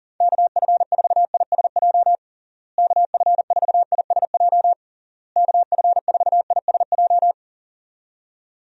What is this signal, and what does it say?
Morse code, heard as a keyed tone, about 37 wpm, 700 Hz. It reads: KF4ISJ KF4ISJ KF4ISJ
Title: USA Call Signs - 6 Characters - Repeated 3x - 40wpm